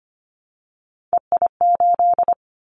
Morse code, heard as a keyed tone, about 25 wpm, 690 Hz. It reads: EI8
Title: Callsign Prefixes for Top 50 DXCC - 25wpm